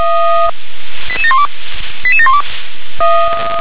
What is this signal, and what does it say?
Five-tone selective call (ZVEI), heard as quick runs of five tones, tones of about 70 ms. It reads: 88712 88712